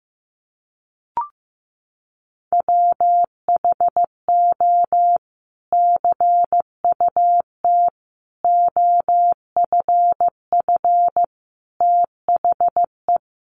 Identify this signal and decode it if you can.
Morse code, heard as a keyed tone, about 15 wpm, 700 Hz. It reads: WHO CUT OFF THE
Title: Sentences from Top 1300 Words 15wpm